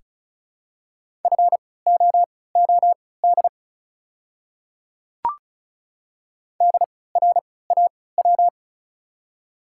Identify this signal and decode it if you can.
Morse code, heard as a keyed tone, about 34 wpm, spaced as for 20 wpm, 700 Hz. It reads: FOOD DRAW
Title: 3x Character Spacing - Top 300 Words - 35wpm